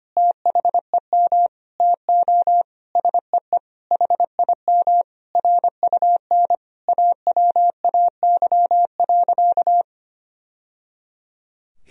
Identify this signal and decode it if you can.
Morse code, heard as a keyed tone, about 25 wpm, 700 Hz. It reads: THEM TO SEE HIM RUN AWAY.
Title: Sentences from Top 1000 Words - Review Part 1 - 25wpm